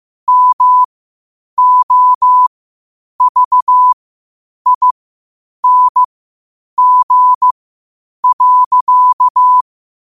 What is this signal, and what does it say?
Morse code, heard as a keyed tone, about 15 wpm, spaced as for 8 wpm, 1000 Hz. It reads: MOVING.
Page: New quotes every day in morse code at 8 Words per minute.